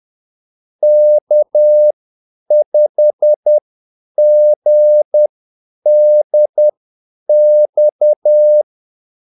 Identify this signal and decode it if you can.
Morse code, heard as a keyed tone, about 10 wpm, spaced as for 8 wpm, 600 Hz. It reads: K5GDX
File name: Callsigns-10-8wpm